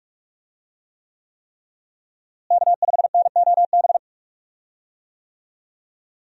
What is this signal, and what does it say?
Morse code, heard as a keyed tone, about 44 wpm, 700 Hz. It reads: K5NOB
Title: USA Call Signs - 5 Characters - Repeated 3x - 45wpm